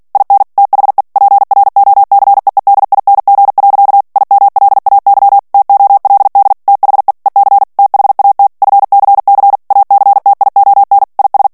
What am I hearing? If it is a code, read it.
Morse code, heard as a keyed tone, about 48 wpm, 800 Hz. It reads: IN THE PROCEEDING. IMLAY TOLD THE EC THAT FCC ACTION IS
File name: ANGLAIS texte clair 3 vitesse 45 mots minute